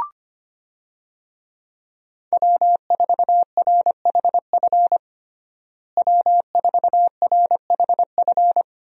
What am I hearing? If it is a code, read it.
Morse code, heard as a keyed tone, about 25 wpm, 700 Hz. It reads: W4RHF W4RHF
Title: USA Call Signs - 5 Characters - Repeated 3x - 25wpm